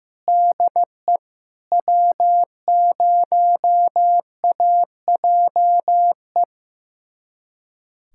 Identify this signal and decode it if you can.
Morse code, heard as a keyed tone, about 15 wpm, 700 Hz. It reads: DE W0AJE
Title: USA Callsign DE Callsign - 5 Characters - 15wpm